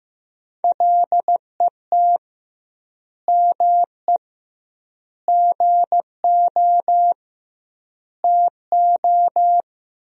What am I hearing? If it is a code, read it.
Morse code, heard as a keyed tone, about 15 wpm, 700 Hz. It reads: LET ME GO TO
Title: 2x Word-Spacing - Sentences from Top 300 Words - 15wpm